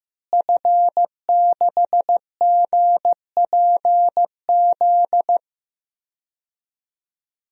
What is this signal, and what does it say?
Morse code, heard as a keyed tone, about 15 wpm, 700 Hz. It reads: F6GPZ
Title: DXCC-100 Call Signs - 5 Characters - Repeated 3x - 15wpm